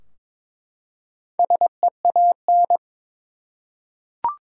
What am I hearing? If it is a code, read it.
Morse code, heard as a keyed tone, about 22 wpm, 700 Hz. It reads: SEAN
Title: USA Common Names - 22wpm